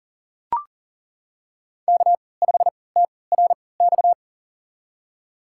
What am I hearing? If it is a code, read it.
Morse code, heard as a keyed tone, about 38 wpm, spaced as for 22 wpm, 700 Hz. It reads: K5TRX
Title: USA Call Signs - 5 Characters - 3x Character Spacing - 40wpm